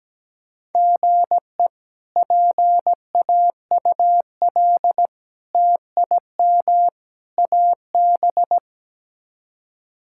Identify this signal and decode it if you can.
Morse code, heard as a keyed tone, about 17 wpm, 700 Hz. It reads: GE PAUL TIM AB
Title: K1USN SST QSO Practice - 17wpm